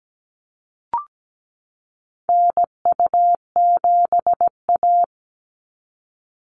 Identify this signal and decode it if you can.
Morse code, heard as a keyed tone, about 17 wpm, 700 Hz. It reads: NU7A
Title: NA SOTA Chaser Callsigns 17wpm